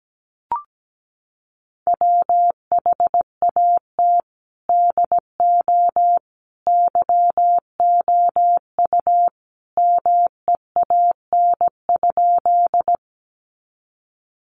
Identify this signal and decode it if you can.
Morse code, heard as a keyed tone, about 17 wpm, 700 Hz. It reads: WHAT DO YOU MEAN?
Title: Sentences from Top 200 Words 17wpm